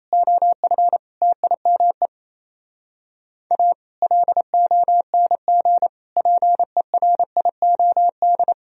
Morse code, heard as a keyed tone, about 27 wpm, 700 Hz. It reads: OF TIME A LONG PERIOD
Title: Speed-Racing - Sets of 5 Words 28wpm